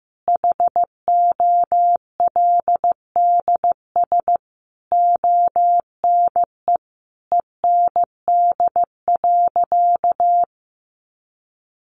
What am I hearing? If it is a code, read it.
Morse code, heard as a keyed tone, about 15 wpm, 700 Hz. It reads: HOLDS ONE END.
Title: Easy Phrases 15wpm